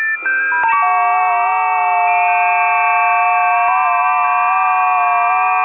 FT8.
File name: tests / 20m busy / test 05